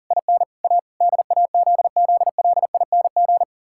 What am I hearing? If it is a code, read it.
Morse code, heard as a keyed tone, about 37 wpm, 700 Hz. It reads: IN A DAZZLING
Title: Pangram Challenge v2 - 40wpm